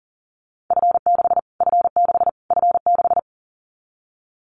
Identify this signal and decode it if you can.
Morse code, heard as a keyed tone, about 40 wpm, 700 Hz. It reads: F6 F6 F6
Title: Callsign Prefixes for Top 25 DXCC - Repeated 3x - 40wpm